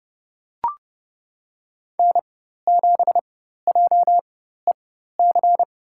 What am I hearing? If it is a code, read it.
Morse code, heard as a keyed tone, about 30 wpm, spaced as for 14 wpm, 700 Hz. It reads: N7JEC
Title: USA Call Signs - 5 Characters - 4x Character Spacing - 30wpm